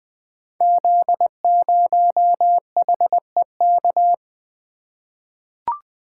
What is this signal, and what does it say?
Morse code, heard as a keyed tone, about 20 wpm, 700 Hz. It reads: Z0HEK
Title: Code Groups - 5 Letter:Numbers - 20wpm